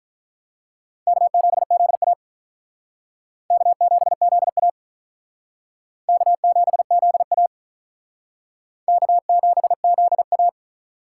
Morse code, heard as a keyed tone, about 47 wpm, 700 Hz. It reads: K7ZA K7ZA K7ZA K7ZA